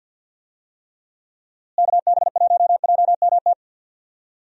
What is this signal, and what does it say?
Morse code, heard as a keyed tone, about 50 wpm, 700 Hz. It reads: KB1JMT
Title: USA Call Signs - 6 Characters - Repeated 3x - 50wpm